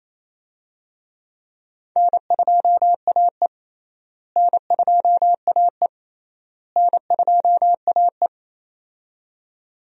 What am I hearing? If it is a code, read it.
Morse code, heard as a keyed tone, about 27 wpm, 700 Hz. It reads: N2AE N2AE N2AE
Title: USA Call Signs - 4 Characters - Repeated 3x - 28wpm